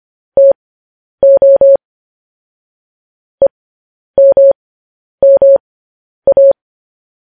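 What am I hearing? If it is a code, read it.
Morse code, heard as a keyed tone, about 25 wpm, spaced as for 10 wpm, 550 Hz. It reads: TO EMMA
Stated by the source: News headlines sent in Morse code at 10 wpm.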